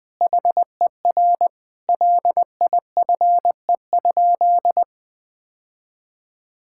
Morse code, heard as a keyed tone, about 20 wpm, 700 Hz. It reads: HER LIFE?
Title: Sentences from Top 1000 Words - Review Part 6 - 20wpm